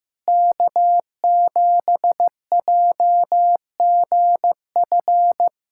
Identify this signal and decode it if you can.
Morse code, heard as a keyed tone, about 16 wpm, 700 Hz. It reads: K7JGF
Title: USA Callsign DE Callsign - 5 Characters - 15wpm